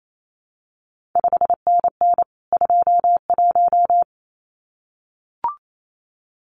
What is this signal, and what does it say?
Morse code, heard as a keyed tone, about 28 wpm, 700 Hz. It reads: 5NN 21
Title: CQ WW DX Contest Exchange Encore - 28wpm